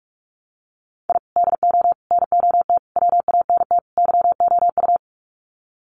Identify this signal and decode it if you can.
Morse code, heard as a keyed tone, about 44 wpm, 700 Hz. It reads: I DO NOT WANT YOU
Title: Sets of 5 Words 45wpm